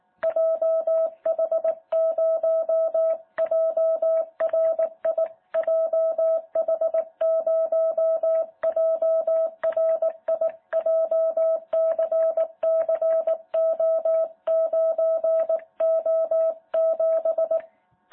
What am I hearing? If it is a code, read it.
Morse code, heard as a keyed tone, about 18 wpm, 650 Hz. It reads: JH0JRI JH0JRI JCCO9O7
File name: jh8cbh / 2021cw